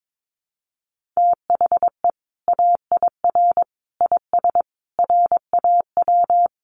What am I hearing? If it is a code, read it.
Morse code, heard as a keyed tone, about 22 wpm, 700 Hz. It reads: THE AIR IS RAW
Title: Sentences from Top 1500 Words 22wpm